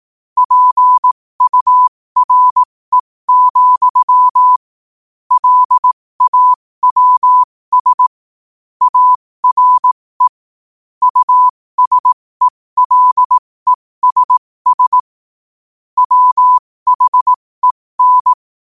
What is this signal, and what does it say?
Morse code, heard as a keyed tone, about 18 wpm, spaced as for 15 wpm, 1000 Hz. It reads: PURE, LAWS ARE USELESS WHEN
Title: 15 WPM morse code quotes for Thu, 29 Feb 2024